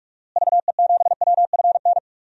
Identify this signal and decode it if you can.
Morse code, heard as a keyed tone, about 45 wpm, 700 Hz. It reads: VE7WFN
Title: DXCC-100 Call Signs - 6 Characters - Repeated 2x - 45wpm